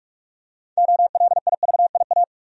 Morse code, heard as a keyed tone, about 46 wpm, 700 Hz. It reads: OLIVIA
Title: USA Common Names - 45wpm